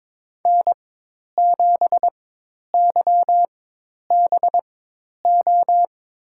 Morse code, heard as a keyed tone, about 22 wpm, spaced as for 10 wpm, 700 Hz. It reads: N7YBO
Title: USA Call Signs - 5 Characters - 4x Character Spacing - 22wpm